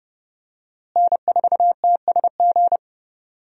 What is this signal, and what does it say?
Morse code, heard as a keyed tone, about 30 wpm, 700 Hz. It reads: N4TSG
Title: USA Call Signs - 5 Characters - Repeated 3x - 30wpm